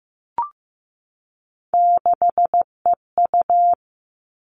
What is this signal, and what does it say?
Morse code, heard as a keyed tone, about 15 wpm, 700 Hz. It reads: 6EU
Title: All Cut Numbers - 3-Digit 15wpm